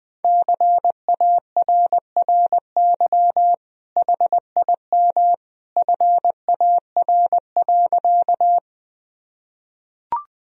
Morse code, heard as a keyed tone, about 21 wpm, 700 Hz. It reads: CARRY HIM FAR.
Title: Sentences from Top 1000 Words - Review Part 1 - 20wpm